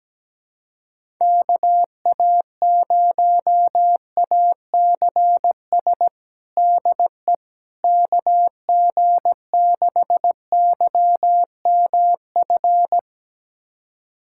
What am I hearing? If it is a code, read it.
Morse code, heard as a keyed tone, about 17 wpm, 700 Hz. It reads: KA0ACS DE KG6YMF
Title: USA Callsign DE Callsign 17wpm